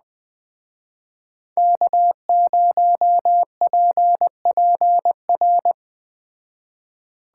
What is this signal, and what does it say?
Morse code, heard as a keyed tone, about 20 wpm, 700 Hz. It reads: K0PPR
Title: Speed Racing - USA 1x3 Call Signs 17wpm